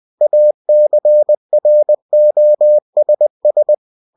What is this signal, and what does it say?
Morse code, heard as a keyed tone, about 20 wpm, 600 Hz. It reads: ACROSS
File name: winnie-20wpm